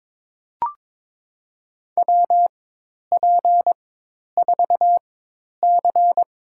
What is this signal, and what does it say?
Morse code, heard as a keyed tone, about 22 wpm, spaced as for 10 wpm, 700 Hz. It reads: WP4C
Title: USA Call Signs - 4x Character Spacing - 22wpm